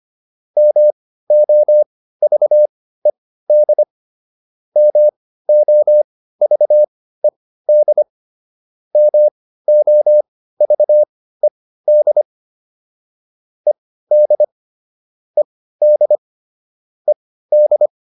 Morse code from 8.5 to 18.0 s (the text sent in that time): MOVED ED ED ED